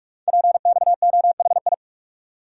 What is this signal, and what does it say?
Morse code, heard as a keyed tone, about 44 wpm, 690 Hz. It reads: PXOHI
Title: Code Groups - 5 Letters - 45wpm